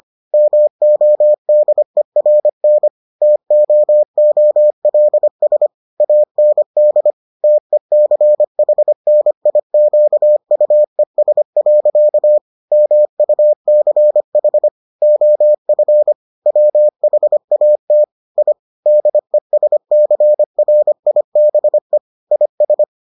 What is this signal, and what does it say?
Morse code, heard as a keyed tone, about 25 wpm, 600 Hz. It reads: MODERN TOOLS AND TECHNIQUES. MUCH OF WHAT I DESCRIBE IS